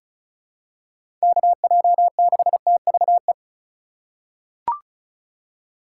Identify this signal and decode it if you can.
Morse code, heard as a keyed tone, about 36 wpm, 700 Hz. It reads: KJ6TVE
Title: USA Call Signs - 6 Characters - 4x Character Spacing - 35wpm